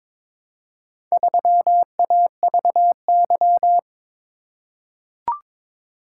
Morse code, heard as a keyed tone, about 22 wpm, 700 Hz. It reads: 3AVY
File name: practice / 22wpm / Four Letter-Number Combinations - 22wpm